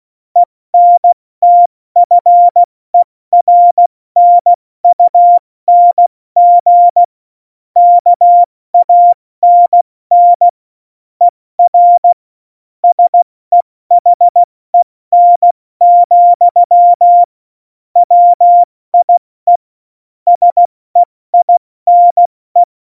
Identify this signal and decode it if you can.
Morse code, heard as a keyed tone, about 16 wpm, spaced as for 14 wpm, 700 Hz. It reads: ENTFERNUNG KANN ER SEHEN, WIE SEINE
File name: Morsekurs / Witze / Witz 13. Oktober 2020, 70 F80